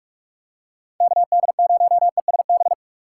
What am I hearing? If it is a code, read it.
Morse code, heard as a keyed tone, about 46 wpm, 700 Hz. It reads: KD0ESB
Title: USA Call Signs - 6 Characters - 6x Character Spacing - 45wpm